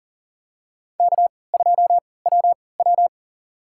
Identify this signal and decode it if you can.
Morse code, heard as a keyed tone, about 37 wpm, spaced as for 22 wpm, 700 Hz. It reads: K2WW
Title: USA Call Signs - 4 Characters - 3x Character Spacing - 40wpm